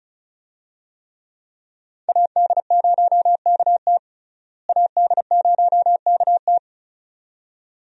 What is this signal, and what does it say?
Morse code, heard as a keyed tone, about 34 wpm, 700 Hz. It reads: AD0KT AD0KT
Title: USA Call Signs - 5 Characters - Repeated 2x - 35wpm